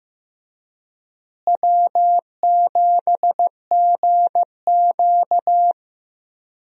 Morse code, heard as a keyed tone, about 15 wpm, 700 Hz. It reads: W7GQ
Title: USA Call Signs - 4 Characters - Repeated 3x - 15wpm